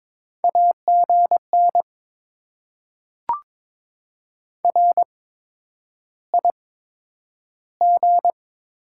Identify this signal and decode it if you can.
Morse code, heard as a keyed tone, about 22 wpm, spaced as for 9 wpm, 700 Hz. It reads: AGN RIG
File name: Top 20 QSO Elements - 8x Character Spacing - 22wpm